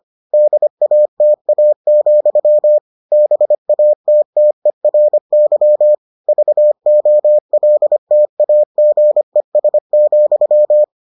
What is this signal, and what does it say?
Morse code, heard as a keyed tone, about 25 wpm, 600 Hz. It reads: DATA, BATTERY VOLTAGES,